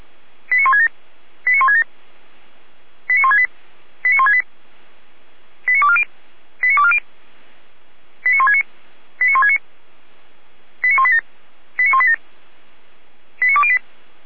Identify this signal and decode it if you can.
Five-tone selective call (ZVEI), heard as quick runs of five tones, tones of about 70 ms. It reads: 89167 89167 89168 89168 89240 89240 89169 89169 89187 89187 89208